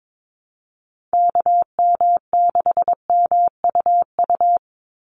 Morse code, heard as a keyed tone, about 22 wpm, 700 Hz. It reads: KM6MUU
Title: USA Call Signs 22wpm